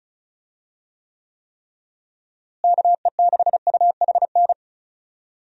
Morse code, heard as a keyed tone, about 34 wpm, 700 Hz. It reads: KE6UHN
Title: USA Call Signs - 6 Characters - Repeated 3x - 35wpm